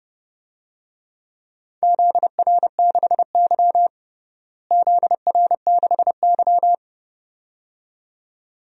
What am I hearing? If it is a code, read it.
Morse code, heard as a keyed tone, about 30 wpm, 700 Hz. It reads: ZR6Y ZR6Y
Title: DXCC-100 Call Signs - 4 Characters - Repeated 2x - 30wpm